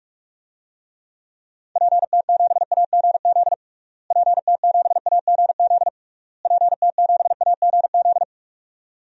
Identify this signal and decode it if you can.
Morse code, heard as a keyed tone, about 46 wpm, 700 Hz. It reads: PT7AGZ PT7AGZ PT7AGZ
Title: DXCC-100 Call Signs - 6 Characters - Repeated 3x - 45wpm